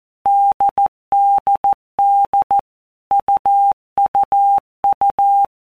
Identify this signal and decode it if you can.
Morse code, heard as a keyed tone, about 14 wpm, 800 Hz. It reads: DDD UUU
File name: Basic Letter Groups By Blocks / Groups@13 / AWUNDG@13